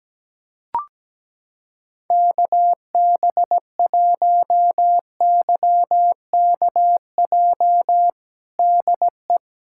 Morse code, heard as a keyed tone, about 17 wpm, 700 Hz. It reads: KB1YKJ DE
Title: USA Callsign DE Callsign - 6 Characters - 17wpm